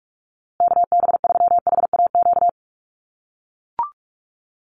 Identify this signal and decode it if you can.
Morse code, heard as a keyed tone, about 44 wpm, 690 Hz. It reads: KB3HAQ
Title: USA Call Signs Repeated 2x 45wpm